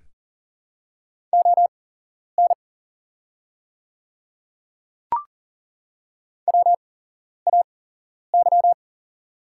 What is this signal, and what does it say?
Morse code, heard as a keyed tone, about 40 wpm, spaced as for 11 wpm, 700 Hz. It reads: ON WAY
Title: 8x Character Spacing - Top 100 Words - 40wpm